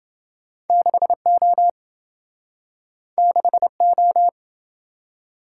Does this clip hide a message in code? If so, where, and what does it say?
Yes: Morse code from 0.7 to 4.3 s, heard as a keyed tone, about 27 wpm, 700 Hz. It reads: 6O 6O